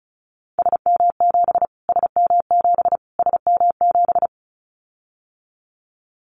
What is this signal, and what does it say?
Morse code, heard as a keyed tone, about 34 wpm, 700 Hz. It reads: SM7 SM7 SM7
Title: Callsign Prefixes for Top 25 DXCC - Repeated 3x - 35wpm